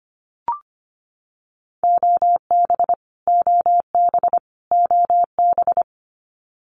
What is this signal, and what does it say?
Morse code, heard as a keyed tone, about 25 wpm, 700 Hz. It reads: OB OB OB
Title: Abbreviations Repeated 3x - 25wpm